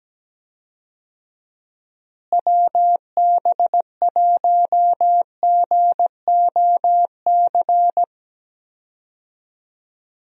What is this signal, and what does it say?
Morse code, heard as a keyed tone, about 17 wpm, 700 Hz. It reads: WB1GOC
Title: USA Call Signs - 6 Characters - Repeated 3x - 17wpm